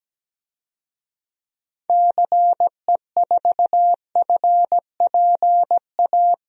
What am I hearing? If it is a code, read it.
Morse code, heard as a keyed tone, about 17 wpm, 700 Hz. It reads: CE4FPA
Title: DXCC-100 Call Signs - 6 Characters - Repeated 2x - 17wpm